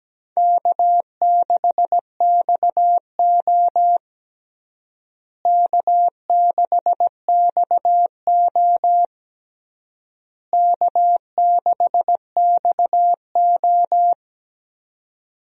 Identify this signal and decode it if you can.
Morse code, heard as a keyed tone, about 17 wpm, 700 Hz. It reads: K6XO K6XO K6XO
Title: USA Call Signs - 4 Characters - Repeated 3x - 17wpm